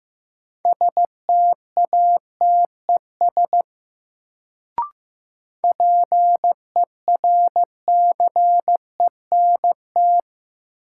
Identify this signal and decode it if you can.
Morse code, heard as a keyed tone, about 15 wpm, 700 Hz. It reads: STATES PERCENT
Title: Sets of 2 Words - New Word Drill - 15wpm